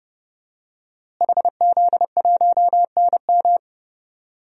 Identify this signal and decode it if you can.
Morse code, heard as a keyed tone, about 30 wpm, 700 Hz. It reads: HZ1NM
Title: DXCC-100 Call Signs - 5 Characters - Repeated 2x - 30wpm